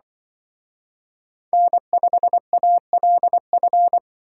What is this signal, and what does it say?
Morse code, heard as a keyed tone, about 24 wpm, 700 Hz. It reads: N5ALF
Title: Speed Racing - USA 1x3 Call Signs 20wpm